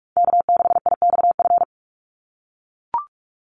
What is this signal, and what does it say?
Morse code, heard as a keyed tone, about 44 wpm, 700 Hz. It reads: K6IXF
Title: USA Call Signs 45wpm